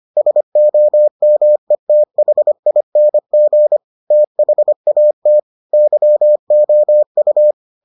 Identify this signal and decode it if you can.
Morse code, heard as a keyed tone, about 25 wpm, 600 Hz. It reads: SOMETHING THAT YOU